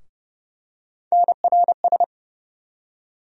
Morse code, heard as a keyed tone, about 32 wpm, 700 Hz. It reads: NRS
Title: USA Call Sign Postfixes - 3 Letters 30wpm